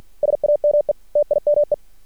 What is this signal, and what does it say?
Morse code, heard as a keyed tone, about 50 wpm, 590 Hz. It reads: SAME TIME
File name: archived / CRSnet